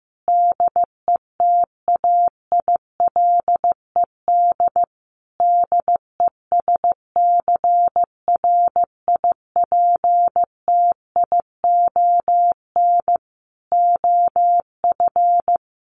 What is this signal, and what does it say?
Morse code, heard as a keyed tone, about 15 wpm, 700 Hz. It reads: DETAILED DESCRIPTION OF
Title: Sentences from Top 1500 Words 15wpm